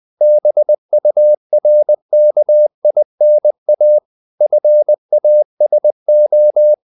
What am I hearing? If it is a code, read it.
Morse code, heard as a keyed tone, about 20 wpm, 600 Hz. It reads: BURKINA FASO
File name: CW / Practice / NationalCapitals20